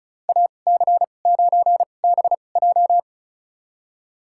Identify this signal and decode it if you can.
Morse code, heard as a keyed tone, about 36 wpm, spaced as for 25 wpm, 700 Hz. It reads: AC9BJ
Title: USA Call Signs - 5 Characters - 2x Character Spacing - 35wpm